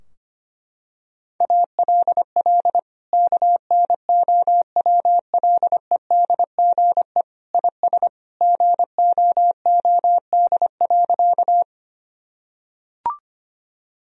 Morse code, heard as a keyed tone, about 25 wpm, 700 Hz. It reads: ALL KNOWLEDGE IS GOOD.
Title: Sentences from Top 1000 Words - Review Part 3 - 25wpm